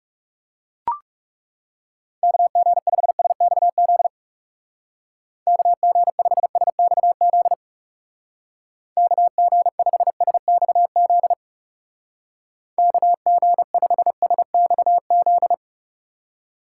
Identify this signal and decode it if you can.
Morse code, heard as a keyed tone, about 37 wpm, 700 Hz. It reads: KG5SXZ KG5SXZ KG5SXZ KG5SXZ
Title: Speed Racing - USA 2x3 Call Signs 30wpm